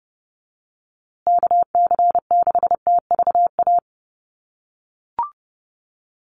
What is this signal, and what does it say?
Morse code, heard as a keyed tone, about 30 wpm, 700 Hz. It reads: KC6TVA
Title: USA Call Signs Repeated 3x 30wpm